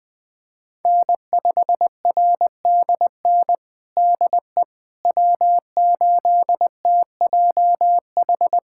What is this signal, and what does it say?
Morse code, heard as a keyed tone, about 20 wpm, 700 Hz. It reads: N5RDN DE W8TJH
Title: USA Callsign DE Callsign - 5 Characters - 20wpm